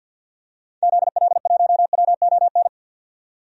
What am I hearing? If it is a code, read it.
Morse code, heard as a keyed tone, about 47 wpm, 700 Hz. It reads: ZL1WON
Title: DXCC-100 Call Signs - 6 Characters - 50wpm